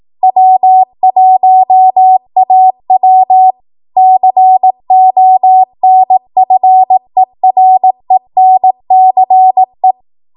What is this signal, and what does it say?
Morse code, heard as a keyed tone, about 18 wpm, 750 Hz. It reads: W1AW CONFERENCE
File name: Archive / 18 WPM / 18WPM